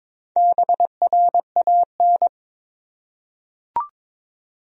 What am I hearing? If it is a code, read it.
Morse code, heard as a keyed tone, about 23 wpm, 700 Hz. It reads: BRAN
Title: USA Common Names - Max 4 Letters - 22wpm